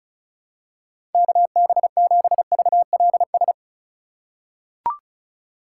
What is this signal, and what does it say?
Morse code, heard as a keyed tone, about 36 wpm, 700 Hz. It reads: KB7VLS
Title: USA Call Signs - 6 Characters - 4x Character Spacing - 35wpm